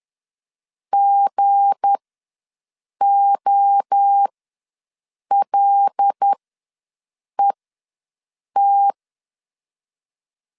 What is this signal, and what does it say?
Morse code, heard as a keyed tone, about 10 wpm, spaced as for 6 wpm, 800 Hz. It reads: GOLET
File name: LEZIONI / 7 GF9Z / 7 G ws 10wpm 3843